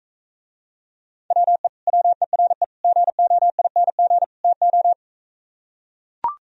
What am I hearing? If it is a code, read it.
Morse code, heard as a keyed tone, about 42 wpm, 690 Hz. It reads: WE WERE GOING TO